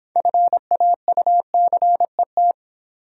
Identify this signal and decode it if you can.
Morse code, heard as a keyed tone, about 26 wpm, 700 Hz. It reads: FAUCET